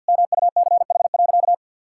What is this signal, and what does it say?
Morse code, heard as a keyed tone, about 50 wpm, 700 Hz. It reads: MUCH.
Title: Sentences from Top 1000 Words - Review Part 5 - 50wpm